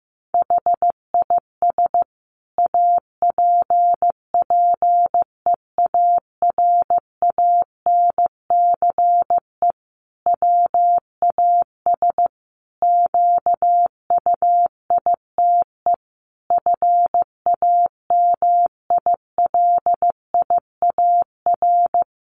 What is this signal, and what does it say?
Morse code, heard as a keyed tone, about 15 wpm, 700 Hz. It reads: HIS APPEARANCE WAS QUITE FAMILIAR
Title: Sentences from Top 1100 Words 15wpm